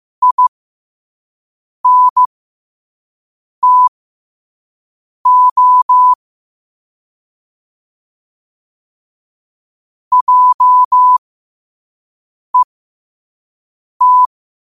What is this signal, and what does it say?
Morse code, heard as a keyed tone, about 15 wpm, spaced as for 5 wpm, 1000 Hz. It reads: INTO JET
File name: qotdfiles / 5wpm